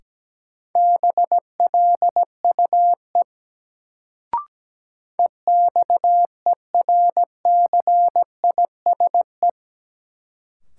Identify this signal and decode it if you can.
Morse code, heard as a keyed tone, about 17 wpm, 700 Hz. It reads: BLUE EXERCISE
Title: Sentences from Top 1200 Words - New Word Drill Alt - 17wpm